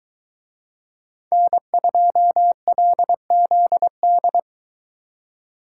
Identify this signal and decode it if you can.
Morse code, heard as a keyed tone, about 22 wpm, 700 Hz. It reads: N2LZD